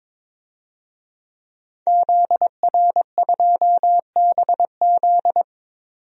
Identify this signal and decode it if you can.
Morse code, heard as a keyed tone, about 22 wpm, 700 Hz. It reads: ZR2BZ
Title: DXCC-100 Call Signs - 5 Characters - 22wpm